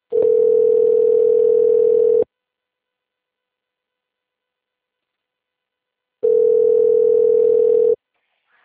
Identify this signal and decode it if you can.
Ringback tone (North America), 440 + 480 Hz.
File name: rg-1600-unknown- - - .73350